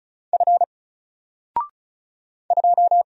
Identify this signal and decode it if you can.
Morse code, heard as a keyed tone, about 36 wpm, 700 Hz. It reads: F 2
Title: Single Character (Letter:Number:Punctuation:Prosign) Alt 35wpm